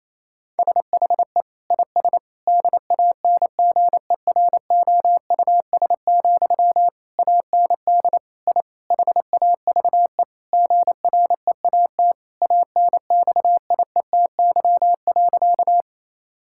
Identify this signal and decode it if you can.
Morse code, heard as a keyed tone, about 28 wpm, 700 Hz. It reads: SHE IS DANGEROUS, AND I HAVE GREAT ANXIETY.
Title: Sentences from Top 1600 Words 28wpm